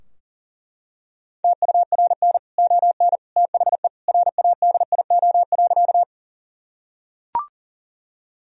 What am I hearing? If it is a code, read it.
Morse code, heard as a keyed tone, about 40 wpm, 700 Hz. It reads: TURN ON THE RADIO.
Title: Sentences from Top 1000 Words - Review Part 3 - 40wpm